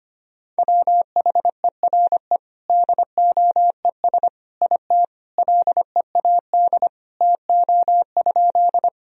Morse code, heard as a keyed tone, about 25 wpm, 700 Hz. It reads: WHERE DOES IT LEAD TO?
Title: Sentences from Top 1000 Words - Review Part 2 - 25wpm